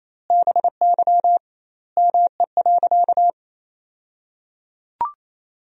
Morse code, heard as a keyed tone, about 28 wpm, 700 Hz. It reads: BY ME.
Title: 2x Word-Spacing - Sentences from Top 500 Words - 28wpm